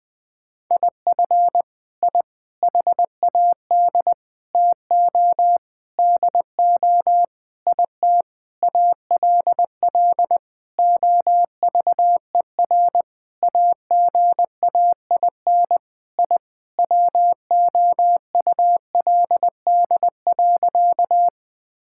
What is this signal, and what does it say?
Morse code, heard as a keyed tone, about 20 wpm, 700 Hz. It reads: IF I HAD TO DO IT ALL OVER AGAIN I WOULD.
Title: Sentences with 9-Word Phrases - 20wpm